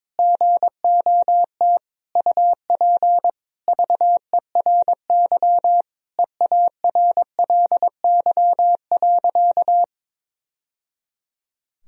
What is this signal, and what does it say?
Morse code, heard as a keyed tone, about 21 wpm, 700 Hz. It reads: GOT UP VERY EARLY.
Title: Sentences from Top 1000 Words - Review Part 2 - 22wpm